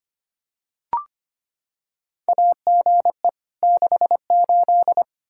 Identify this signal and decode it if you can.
Morse code, heard as a keyed tone, about 25 wpm, 700 Hz. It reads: AGE 68
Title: Practice QSO - Exchange 3 - More Info - 25wpm